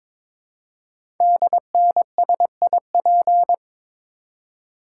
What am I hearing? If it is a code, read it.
Morse code, heard as a keyed tone, about 23 wpm, 700 Hz. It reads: DNSIP
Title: Code Groups - 5 Letters - 22wpm